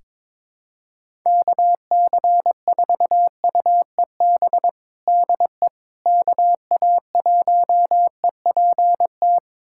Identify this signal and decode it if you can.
Morse code, heard as a keyed tone, about 22 wpm, 700 Hz. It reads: KC4UEB DE KA1EPT
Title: USA Callsign DE Callsign - 6 Characters - 22wpm